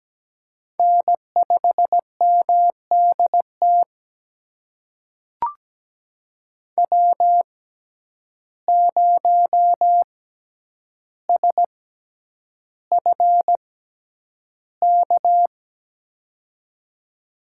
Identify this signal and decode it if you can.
Morse code, heard as a keyed tone, about 17 wpm, spaced as for 9 wpm, 700 Hz. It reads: N5MDT W0SFK
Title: USA Call Signs - 6x Character Spacing - 17wpm